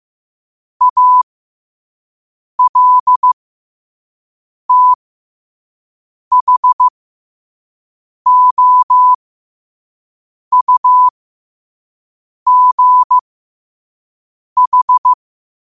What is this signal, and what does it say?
Morse code, heard as a keyed tone, about 15 wpm, spaced as for 5 wpm, 1000 Hz. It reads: ALTHOUGH
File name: qotdfiles / 5wpm